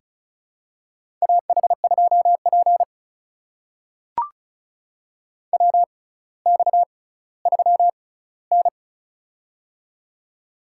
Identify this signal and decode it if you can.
Morse code, heard as a keyed tone, about 36 wpm, spaced as for 18 wpm, 700 Hz. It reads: AH2P WX3N
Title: USA Call Signs - 6x Character Spacing - 35wpm